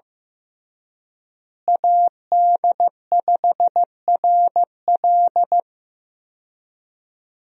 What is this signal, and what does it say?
Morse code, heard as a keyed tone, about 15 wpm, 700 Hz. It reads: AD5RL
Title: USA Call Signs - 5 Characters - Repeated 3x - 15wpm